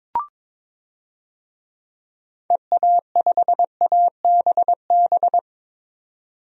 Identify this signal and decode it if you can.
Morse code, heard as a keyed tone, about 22 wpm, 700 Hz. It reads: EA5ABB
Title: DXCC-100 Call Signs - 6 Characters - 22wpm